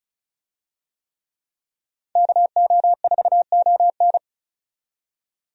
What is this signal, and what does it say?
Morse code, heard as a keyed tone, about 36 wpm, 700 Hz. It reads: KO4ON
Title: USA Call Signs - 5 Characters - 35wpm